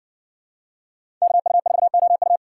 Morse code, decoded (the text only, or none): KU4OU